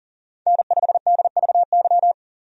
Morse code, heard as a keyed tone, about 38 wpm, 700 Hz. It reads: N5DVY
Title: USA Call Signs - 5 Characters - 40wpm